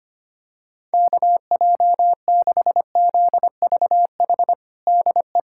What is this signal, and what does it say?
Morse code, heard as a keyed tone, about 25 wpm, 700 Hz. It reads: KJ6ZVH DE
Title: USA Callsign DE Callsign 25wpm